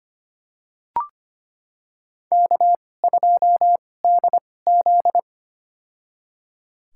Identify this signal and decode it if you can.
Morse code, heard as a keyed tone, about 25 wpm, spaced as for 18 wpm, 700 Hz. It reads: K2DZ
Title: USA Call Signs - 4 Characters - 2x Character Spacing - 25wpm